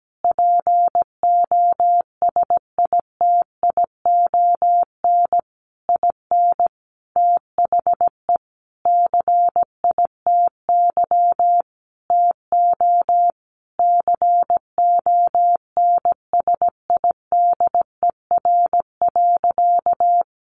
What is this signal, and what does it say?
Morse code, heard as a keyed tone, about 17 wpm, 700 Hz. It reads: POSITION IN THE CITY TO CONSIDER.
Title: Sentences from Top 900 Words 17wpm